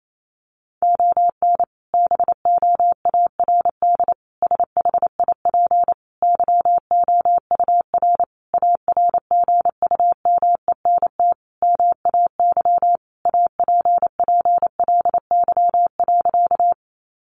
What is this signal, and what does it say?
Morse code, heard as a keyed tone, about 28 wpm, 700 Hz. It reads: ON BOARD SHIP YOUR ARGUMENT MAY APPLY.
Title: Sentences from Top 1400 Words 28wpm